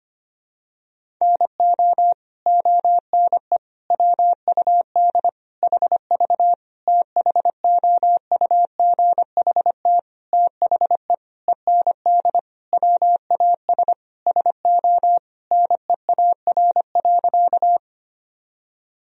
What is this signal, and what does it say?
Morse code, heard as a keyed tone, about 25 wpm, 700 Hz. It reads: NO ONE WUD HV THOUGHT THE END WAS SO NEAR.
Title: Sentences from Top 300 Words and Abbreviations 25wpm